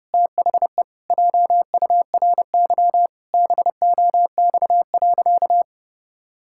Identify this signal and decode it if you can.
Morse code, heard as a keyed tone, about 30 wpm, 700 Hz. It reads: THE JURY BOX.
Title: Pangram Challenge v2 - 30wpm